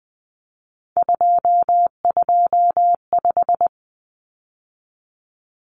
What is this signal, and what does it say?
Morse code, heard as a keyed tone, about 20 wpm, 700 Hz. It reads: 225
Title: Common Cut Numbers - 3-Digit 20wpm